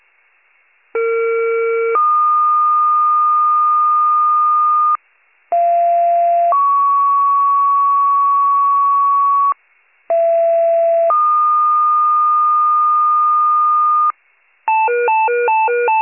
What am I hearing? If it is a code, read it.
Two-tone page, heard as a one-second tone, then a three-second tone, about 460 then 1220 Hz, about 690 then 1120 Hz, about 670 then 1220 Hz. It is heard from 0.9 to 14.1 s.